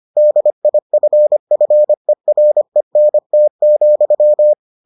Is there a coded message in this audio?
Yes: Morse code from 0.2 to 4.5 s, heard as a keyed tone, about 25 wpm, 600 Hz. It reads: DIFFERENT,